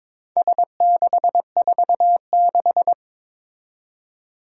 Morse code, heard as a keyed tone, about 22 wpm, 700 Hz. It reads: S646
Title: Four Challenging Letter-Number Combinations 22wpm